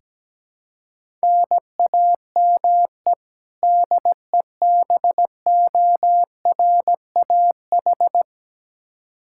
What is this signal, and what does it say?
Morse code, heard as a keyed tone, about 17 wpm, 700 Hz. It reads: NAME DEBORAH
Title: Practice QSO - Exchange 1 - Beginning - 17wpm